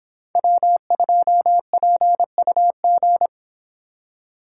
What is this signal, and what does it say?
Morse code, heard as a keyed tone, about 26 wpm, 700 Hz. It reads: W2PUG